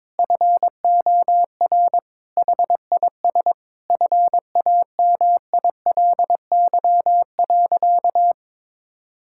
Morse code, heard as a keyed tone, about 22 wpm, 700 Hz. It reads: FOR HIS FAMILY.
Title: Sentences from Top 1000 Words - Review Part 5 - 22wpm